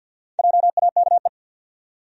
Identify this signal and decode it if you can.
Morse code, heard as a keyed tone, about 51 wpm, 700 Hz. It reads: JAKE
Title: USA Common Names - Max 4 Letters - 50wpm